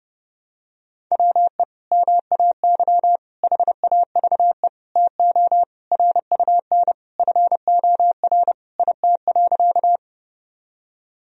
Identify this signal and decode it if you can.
Morse code, heard as a keyed tone, about 30 wpm, 700 Hz. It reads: WE MAY HAVE TO RUN FOR IT.
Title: Sentences from Top 1000 Words - Review Part 1 - 30wpm